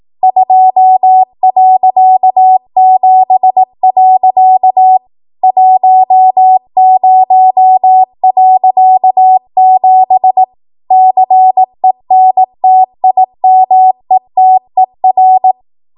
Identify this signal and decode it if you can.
Morse code, heard as a keyed tone, about 18 wpm, 750 Hz. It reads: 2.7. 10.7 CENTIMETER
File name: Morse / 18WPM